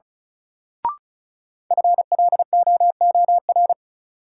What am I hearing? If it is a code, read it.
Morse code, heard as a keyed tone, about 34 wpm, 700 Hz. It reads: FLOOR
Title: Sets of 3 Words - New Word Drill - 35wpm